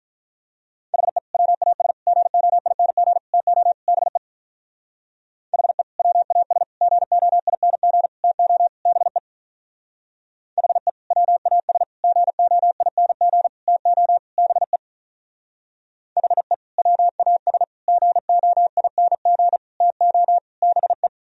Morse code, heard as a keyed tone, about 44 wpm, 700 Hz. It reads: HE WAS GOING TO BE HE WAS GOING TO BE HE WAS GOING TO BE HE WAS GOING TO BE